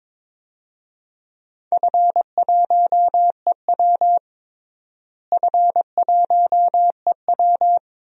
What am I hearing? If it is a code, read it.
Morse code, heard as a keyed tone, about 22 wpm, 700 Hz. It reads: F1EW F1EW
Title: DXCC-100 Call Signs - 4 Characters - Repeated 3x - 22wpm